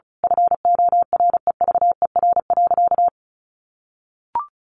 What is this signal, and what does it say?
Morse code, heard as a keyed tone, about 34 wpm, 700 Hz. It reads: FOREVER.
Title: A Sound Of Thunder 35wpm